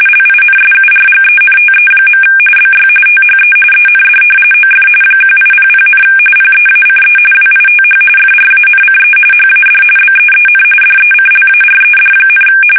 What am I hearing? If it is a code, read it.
RTTY, about 45.45 baud. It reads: LINSURGENTE. BY THE END OF 1800, PEACE WITH FRANCE HAD BEEN DECLARED,
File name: fsk850 / usnavy1